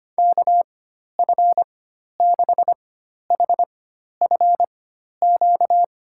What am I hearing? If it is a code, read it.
Morse code, heard as a keyed tone, about 25 wpm, spaced as for 12 wpm, 700 Hz. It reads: KF6HFQ
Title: USA Call Signs - 4x Character Spacing - 25wpm